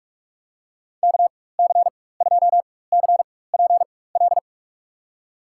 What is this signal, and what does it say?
Morse code, heard as a keyed tone, about 42 wpm, spaced as for 20 wpm, 700 Hz. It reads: KC2CPL
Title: USA Call Signs - 6 Characters - 4x Character Spacing - 45wpm